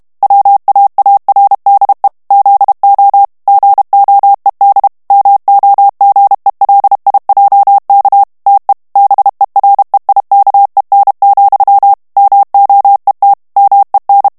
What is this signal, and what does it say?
Morse code, heard as a keyed tone, about 32 wpm, 800 Hz. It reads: WAARDE ZO GOED MOGELIJK TE BEREIKEN, MOET MEN
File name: morse mp3 / HOLLANDAIS texte clair 1 vitesse 30 mots minute